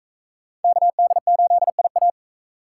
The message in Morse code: KD8IA

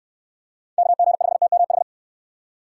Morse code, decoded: KG5TMH

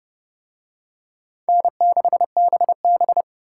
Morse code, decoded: N6BB

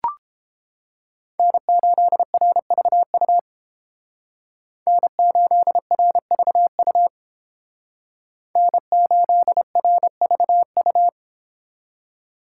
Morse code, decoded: N8RVU N8RVU N8RVU

